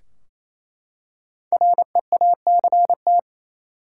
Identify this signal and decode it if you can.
Morse code, heard as a keyed tone, about 28 wpm, 700 Hz. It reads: REACT
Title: Five Letter Words - 28wpm